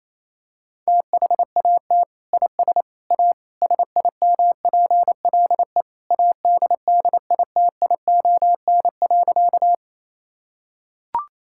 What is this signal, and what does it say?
Morse code, heard as a keyed tone, about 28 wpm, 700 Hz. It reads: THAT IS A SIMPLE ADDITION.
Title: Sentences from Top 1000 Words - Review Part 6 - 28wpm